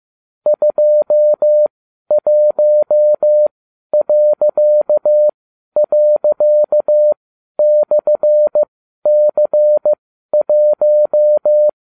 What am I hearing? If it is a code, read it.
Morse code, heard as a keyed tone, about 15 wpm, spaced as for 11 wpm, 600 Hz. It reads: 21../C1